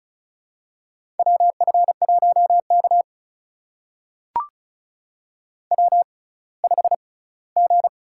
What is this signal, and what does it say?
Morse code, heard as a keyed tone, about 36 wpm, spaced as for 20 wpm, 700 Hz. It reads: WF1K W5G
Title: USA Call Signs - 4 Characters - 6x Character Spacing - 35wpm